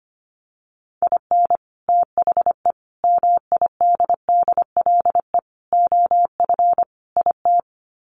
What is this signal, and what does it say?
Morse code, heard as a keyed tone, about 25 wpm, 700 Hz. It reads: IN THE MIDDLE OF IT
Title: Sets of 5 Words Encore 25wpm